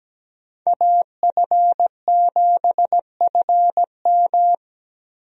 Morse code, heard as a keyed tone, about 17 wpm, 700 Hz. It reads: AF7FM
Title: USA Call Signs - 5 Characters - 17wpm